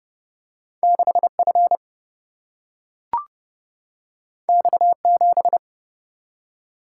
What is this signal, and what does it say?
Morse code, heard as a keyed tone, about 30 wpm, 700 Hz. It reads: 6F X7
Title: Single-Letter Single-Number 30wpm